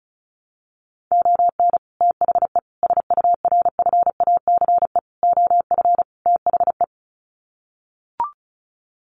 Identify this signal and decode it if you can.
Morse code, heard as a keyed tone, about 34 wpm, 700 Hz. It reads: ON THE SURFACE OF THE
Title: Sets of 5 Words Encore 35wpm